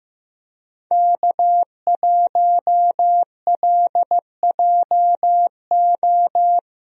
Morse code, heard as a keyed tone, about 15 wpm, 700 Hz. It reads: K1LJO
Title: USA Call Signs - 5 Characters - Repeated 2x - 15wpm